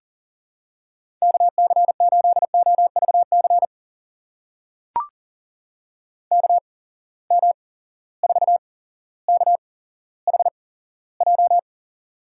Morse code, heard as a keyed tone, about 40 wpm, spaced as for 17 wpm, 700 Hz. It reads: KC8OVC KM4KHJ
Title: USA Call Signs - 6 Characters - 8x Character Spacing - 40wpm